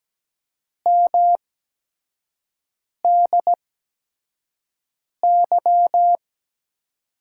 Morse code, heard as a keyed tone, about 17 wpm, spaced as for 5 wpm, 700 Hz. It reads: MDY
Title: USA Call Signs - 6 Characters - 8x Character Spacing - 17wpm